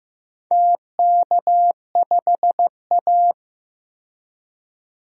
Morse code, heard as a keyed tone, about 15 wpm, 700 Hz. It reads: TK5A
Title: DXCC-100 Call Signs - 4 Characters - 15wpm